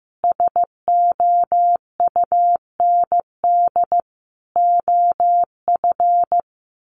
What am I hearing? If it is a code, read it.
Morse code, heard as a keyed tone, about 15 wpm, 700 Hz. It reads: SOUND OF
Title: Sentences from Top 1100 Words 15wpm